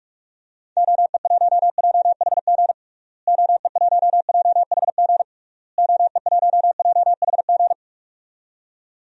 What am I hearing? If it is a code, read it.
Morse code, heard as a keyed tone, about 46 wpm, 700 Hz. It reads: OE1JHG OE1JHG OE1JHG
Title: DXCC-100 Call Signs - 6 Characters - Repeated 3x - 45wpm